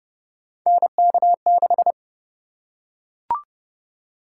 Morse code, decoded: NK6

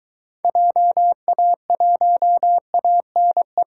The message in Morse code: JA1ANE

